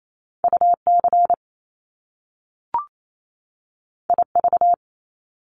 Morse code, heard as a keyed tone, about 28 wpm, 700 Hz. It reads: UC IV
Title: Top Two Letter Combinations in Words 28wpm